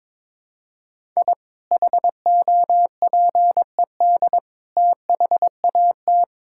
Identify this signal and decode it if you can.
Morse code, heard as a keyed tone, about 22 wpm, 700 Hz. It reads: I HOPED THAT
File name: practice / 22wpm / Sentences with 4-Word Phrases - 22wpm